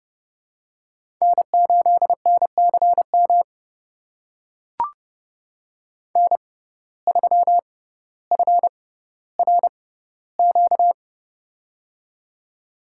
Morse code, heard as a keyed tone, about 30 wpm, spaced as for 15 wpm, 700 Hz. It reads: N8NCM N3FRQ
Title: USA Call Signs - 5 Characters - 6x Character Spacing - 30wpm